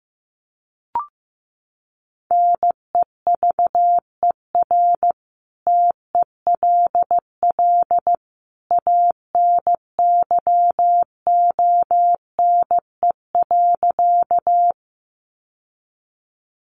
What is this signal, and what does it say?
Morse code, heard as a keyed tone, about 15 wpm, 700 Hz. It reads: NEVER TELL ANYONE.
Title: Sentences from Top 900 Words 15wpm